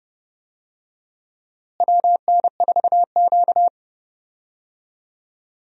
Morse code, heard as a keyed tone, about 29 wpm, 700 Hz. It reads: WN4Q
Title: USA Call Signs - 4 Characters - Repeated 3x - 30wpm